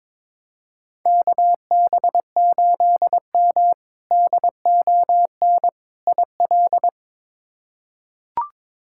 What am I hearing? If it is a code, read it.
Morse code, heard as a keyed tone, about 21 wpm, 700 Hz. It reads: KB8M DON IL
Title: K1USN SST QSO Practice - 22wpm